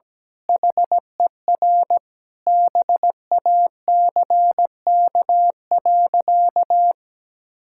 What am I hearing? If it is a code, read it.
Morse code, heard as a keyed tone, about 17 wpm, 700 Hz. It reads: HER BACK.
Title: Sentences from Top 1000 Words - Review Part 6 - 17wpm